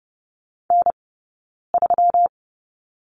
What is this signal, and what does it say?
Morse code, heard as a keyed tone, about 30 wpm, 700 Hz. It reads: N 3
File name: Lesson 20d - Number 3 - Two Characters 30wpm